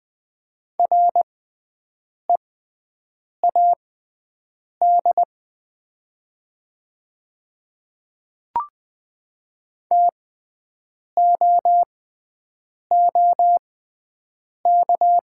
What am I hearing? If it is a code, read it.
Morse code, heard as a keyed tone, about 20 wpm, spaced as for 7 wpm, 700 Hz. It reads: READ TOOK
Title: 6x Character Spacing - Top 500 Words - 20wpm